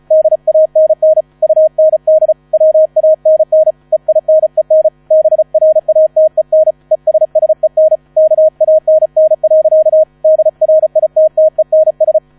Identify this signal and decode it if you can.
Morse code, heard as a keyed tone, about 36 wpm, 630 Hz. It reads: DANN UND WANN EINEN BRATEN ESSEN KANN. DRITTENS
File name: 35wpm